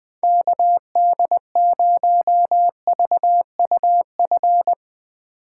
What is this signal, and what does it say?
Morse code, heard as a keyed tone, about 20 wpm, 700 Hz. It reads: KD0VUF
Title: Speed Racing - USA 2x3 Call Signs 20wpm